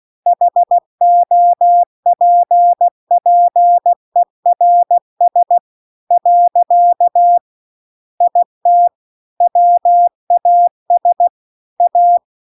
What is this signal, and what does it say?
Morse code, heard as a keyed tone, about 16 wpm, 700 Hz. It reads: HOPPERS . IT WAS A
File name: 16WPM